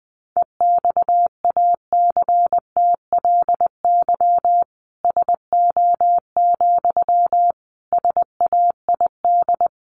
Morse code, heard as a keyed tone, about 19 wpm, 700 Hz. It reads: EXACTLY SO, SAID
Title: Sentences from Top 900 Words 20wpm